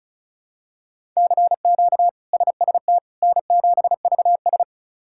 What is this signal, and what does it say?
Morse code, heard as a keyed tone, about 34 wpm, 700 Hz. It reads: CQ SST N7VS
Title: K1USN SST QSO Practice - 35wpm